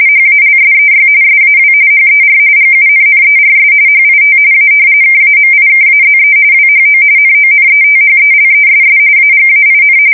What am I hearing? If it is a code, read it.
RTTY, about 45.45 baud. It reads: HALSEY SAW HER GUNNERS HIT A KAMIKAZE WHO NONETHELESS AGAIN
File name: fsk170 / battleshipnj